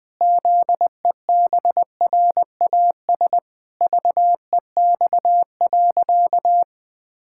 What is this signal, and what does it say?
Morse code, heard as a keyed tone, about 20 wpm, 700 Hz. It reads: ZEBRAS VEX.
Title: Pangram Challenge v2 - 20wpm